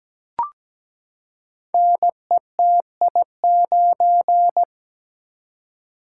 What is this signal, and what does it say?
Morse code, heard as a keyed tone, about 17 wpm, 700 Hz. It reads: NETI9
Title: Code Groups - 5 Letter:Numbers - 17wpm